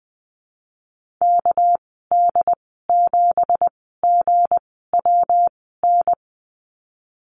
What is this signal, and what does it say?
Morse code, heard as a keyed tone, about 20 wpm, spaced as for 15 wpm, 700 Hz. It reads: KD7GWN
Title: 200 Lesson 32g - Letter k - Call Signs 20wpm